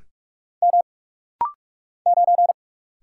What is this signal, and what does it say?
Morse code, heard as a keyed tone, about 46 wpm, 700 Hz. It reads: M 9
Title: Single Letter-Number - Rapid-Fire 45wpm